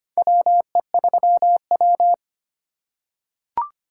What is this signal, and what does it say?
Morse code, heard as a keyed tone, about 25 wpm, 700 Hz. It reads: WE3W
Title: USA Call Signs - 2x Character Spacing - 25wpm